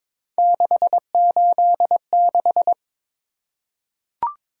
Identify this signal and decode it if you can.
Morse code, heard as a keyed tone, about 23 wpm, 700 Hz. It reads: 686
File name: practice / 22wpm / Three Letter-Number Combinations - 22wpm